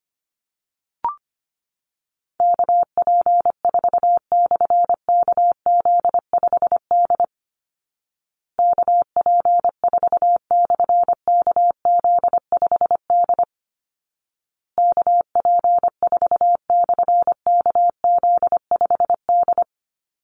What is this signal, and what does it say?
Morse code, heard as a keyed tone, about 25 wpm, 700 Hz. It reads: KP4/KZ5D KP4/KZ5D KP4/KZ5D
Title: Call Signs Repeated 2x - 25wpm